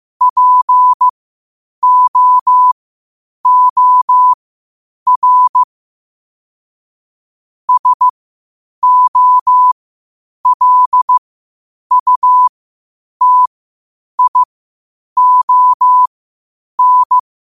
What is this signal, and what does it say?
Morse code, heard as a keyed tone, about 15 wpm, spaced as for 8 wpm, 1000 Hz. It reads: POOR SOLUTION
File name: 8wpm